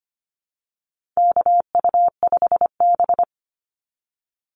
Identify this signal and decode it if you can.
Morse code, heard as a keyed tone, about 25 wpm, 700 Hz. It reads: KU5B
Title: Call Signs Repeated 2x - 25wpm